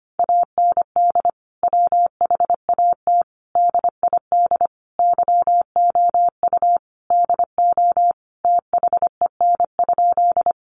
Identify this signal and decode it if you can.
Morse code, heard as a keyed tone, about 25 wpm, 700 Hz. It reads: AND WHAT DID YOU DO THEN?
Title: Sentences from Top 100 Words 25wpm